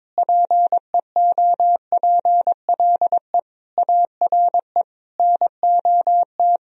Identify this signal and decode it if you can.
Morse code, heard as a keyed tone, about 22 wpm, 700 Hz. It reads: PEOPLE ARE NOT
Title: Sentences from Top 1000 Words - Review Part 3 - 22wpm